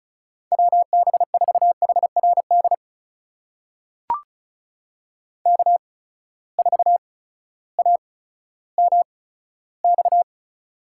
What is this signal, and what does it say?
Morse code, heard as a keyed tone, about 36 wpm, spaced as for 16 wpm, 700 Hz. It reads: WB4HRD K4AMX
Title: USA Call Signs - 8x Character Spacing - 35wpm